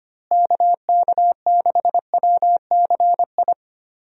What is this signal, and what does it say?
Morse code, heard as a keyed tone, about 25 wpm, 700 Hz. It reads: KK6WCI
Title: USA Call Signs - 6 Characters - Repeated 2x - 25wpm